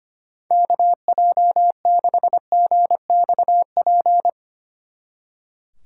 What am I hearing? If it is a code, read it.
Morse code, heard as a keyed tone, about 31 wpm, 700 Hz. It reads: KJ6GXP
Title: Speed Racing - USA 2x3 Call Signs 25wpm